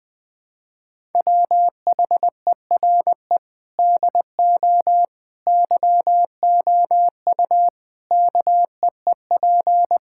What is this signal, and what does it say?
Morse code, heard as a keyed tone, about 20 wpm, 700 Hz. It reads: WHERE DO YOU KEEP
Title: Sentences from Top 1000 Words - Review Part 5 - 20wpm